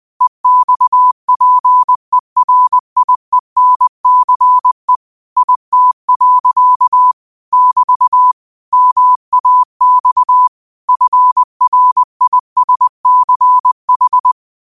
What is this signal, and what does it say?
Morse code, heard as a keyed tone, about 20 wpm, 1000 Hz. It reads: EXPERIENCE IT. <BT> MAX FRISCH
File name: qotdfiles / 20wpm